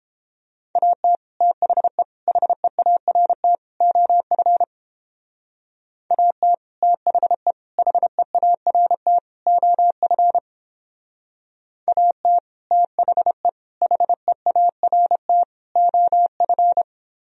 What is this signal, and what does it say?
Morse code, heard as a keyed tone, about 29 wpm, 700 Hz. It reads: AT THE HEART OF AT THE HEART OF AT THE HEART OF